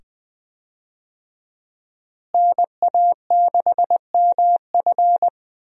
Morse code, heard as a keyed tone, about 20 wpm, 700 Hz. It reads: NA6MF
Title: USA Call Signs - 5 Characters - Repeated 2x - 20wpm